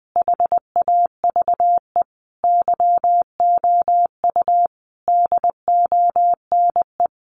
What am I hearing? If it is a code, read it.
Morse code, heard as a keyed tone, about 19 wpm, 700 Hz. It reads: HAVE YOU DONE
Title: Sentences from Top 500 Words 20wpm